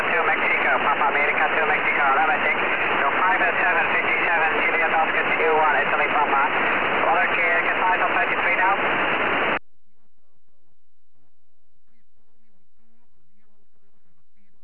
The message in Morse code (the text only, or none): none